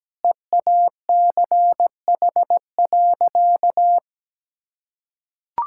Morse code, heard as a keyed tone, about 17 wpm, 700 Hz. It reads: EACH.
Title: Sentences from Top 1000 Words - Review Part 5 - 17wpm